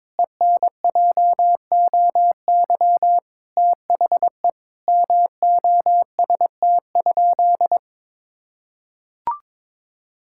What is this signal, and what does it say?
Morse code, heard as a keyed tone, about 22 wpm, 700 Hz. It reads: ENJOY THE MOST?
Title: Sentences from Top 1000 Words - Review Part 5 - 22wpm